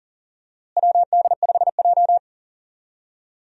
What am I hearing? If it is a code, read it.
Morse code, heard as a keyed tone, about 40 wpm, 700 Hz. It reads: WD5J
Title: USA Call Signs - 4 Characters - Repeated 2x - 40wpm